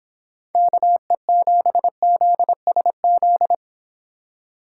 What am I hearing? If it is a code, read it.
Morse code, heard as a keyed tone, about 26 wpm, 700 Hz. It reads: KE7ZSZ
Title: Speed Racing - USA 2x3 Call Signs 22wpm